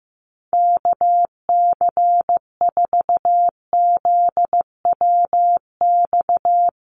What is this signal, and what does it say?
Morse code, heard as a keyed tone, about 15 wpm, 700 Hz. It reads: KC4ZWX
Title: USA Call Signs 15wpm